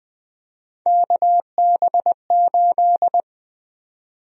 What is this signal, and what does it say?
Morse code, heard as a keyed tone, about 21 wpm, 700 Hz. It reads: KB8
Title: USA Call Sign Prefixes - 20wpm